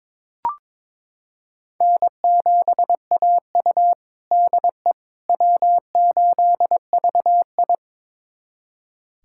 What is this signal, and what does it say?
Morse code, heard as a keyed tone, about 22 wpm, 700 Hz. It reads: N7AU DE W8VI
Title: USA Callsign DE Callsign - 4 Characters - 22wpm